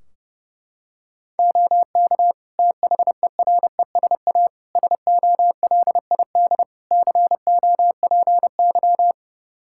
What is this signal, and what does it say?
Morse code, heard as a keyed tone, about 31 wpm, 700 Hz. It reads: OK THERESA SOLID COPY
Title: Practice QSO - Exchange 3 - More Info - 30wpm